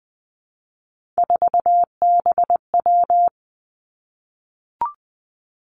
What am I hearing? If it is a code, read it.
Morse code, heard as a keyed tone, about 20 wpm, 700 Hz. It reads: 4BW
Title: Three Challenging Letter-Number Combinations 20wpm